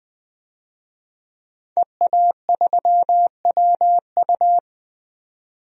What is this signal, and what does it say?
Morse code, heard as a keyed tone, about 20 wpm, 700 Hz. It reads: EA3WU
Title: DXCC-100 Call Signs - 5 Characters - 20wpm